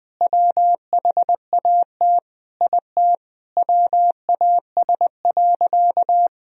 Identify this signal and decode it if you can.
Morse code, heard as a keyed tone, about 21 wpm, 700 Hz. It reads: WHAT IT WAS.
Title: Sentences from Top 1000 Words - Review Part 3 - 20wpm